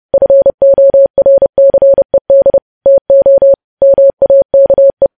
Morse code, heard as a keyed tone, about 30 wpm, 550 Hz. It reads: FORCED TO MAKE
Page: news headlines in morse code at 30 wpm